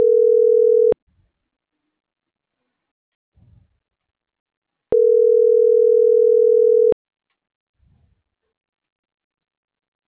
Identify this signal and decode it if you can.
Ringback tone (North America), 440 + 480 Hz.